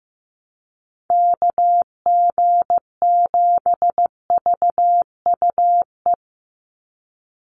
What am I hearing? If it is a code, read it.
Morse code, heard as a keyed tone, about 15 wpm, 700 Hz. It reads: KG7VUE
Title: USA Call Signs 15wpm